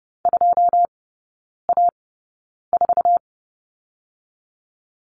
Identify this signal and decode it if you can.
Morse code, heard as a keyed tone, about 32 wpm, spaced as for 9 wpm, 700 Hz. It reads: 2A4
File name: Lesson 17e - Letter u - Three Characters 30wpm